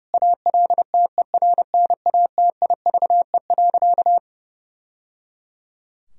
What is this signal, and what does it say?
Morse code, heard as a keyed tone, about 30 wpm, 700 Hz. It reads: ALTERNATIVE.
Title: Sentences from Top 1000 Words - Review Part 4 - 30wpm